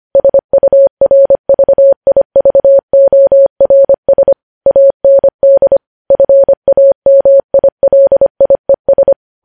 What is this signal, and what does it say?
Morse code, heard as a keyed tone, about 25 wpm, 550 Hz. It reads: SURVIVORS AND FAMILIES
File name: podcast / News headlines-1x-25x25